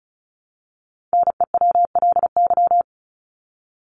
Morse code, heard as a keyed tone, about 34 wpm, 700 Hz. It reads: NEWLY
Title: LY Suffix - Top 100 Words with Base Words 35wpm